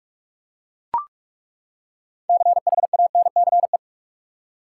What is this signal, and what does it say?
Morse code, heard as a keyed tone, about 41 wpm, 700 Hz. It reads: CHANCE
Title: Six Letter Words - 45wpm